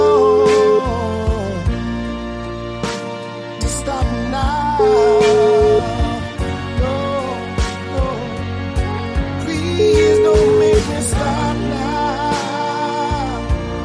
Ringback tone (Europe, 425 Hz), with noise.